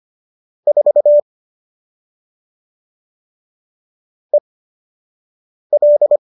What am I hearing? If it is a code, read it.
Morse code, heard as a keyed tone, about 25 wpm, spaced as for 6 wpm, 570 Hz. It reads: 4 EL